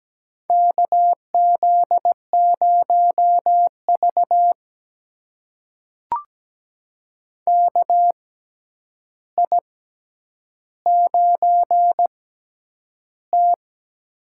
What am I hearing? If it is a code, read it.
Morse code, heard as a keyed tone, about 17 wpm, spaced as for 9 wpm, 700 Hz. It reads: KZ0V KI9T
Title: USA Call Signs - 4 Characters - 6x Character Spacing - 17wpm